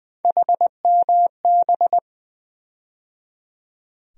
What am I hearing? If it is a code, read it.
Morse code, heard as a keyed tone, about 20 wpm, 700 Hz. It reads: HMB
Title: USA Call Sign Postfixes - 3 Letters 20wpm